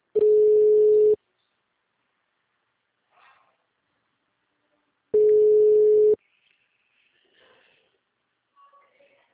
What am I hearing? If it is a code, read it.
Ringback tone (Europe, 425 Hz).